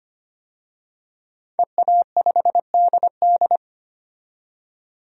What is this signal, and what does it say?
Morse code, heard as a keyed tone, about 25 wpm, 700 Hz. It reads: EA5DD
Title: DXCC-100 Call Signs - 5 Characters - 25wpm